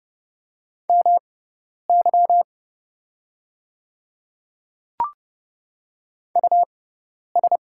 Morse code, heard as a keyed tone, about 30 wpm, spaced as for 10 wpm, 700 Hz. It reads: MY US
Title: 6x Character Spacing - Top 500 Words - 30wpm